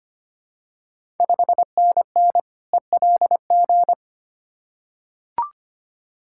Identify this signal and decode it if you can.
Morse code, heard as a keyed tone, about 25 wpm, 700 Hz. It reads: 5NN ELG
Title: Ontario QSO Party Exchange 25wpm